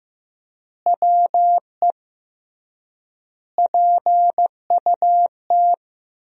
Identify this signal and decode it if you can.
Morse code, heard as a keyed tone, about 16 wpm, 700 Hz. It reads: WE PUT
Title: 3x Word-Spacing - Sentences from Top 200 Words - 15wpm